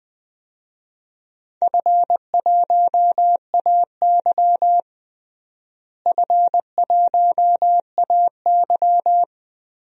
Morse code, heard as a keyed tone, about 20 wpm, 700 Hz. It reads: F1AY F1AY
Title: DXCC-100 Call Signs - 4 Characters - Repeated 3x - 20wpm